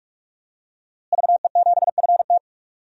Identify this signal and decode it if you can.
Morse code, heard as a keyed tone, about 46 wpm, 700 Hz. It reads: VE7FT
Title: DXCC-100 Call Signs - 5 Characters - Repeated 2x - 45wpm